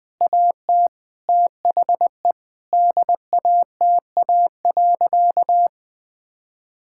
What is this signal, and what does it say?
Morse code, heard as a keyed tone, about 20 wpm, 700 Hz. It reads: AT THE DATA.
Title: Sentences with 7-Word Phrases - 20wpm